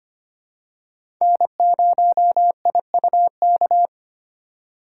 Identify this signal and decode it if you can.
Morse code, heard as a keyed tone, about 25 wpm, 700 Hz. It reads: N0IUK
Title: USA Call Signs - 5 Characters - 3x Character Spacing - 25wpm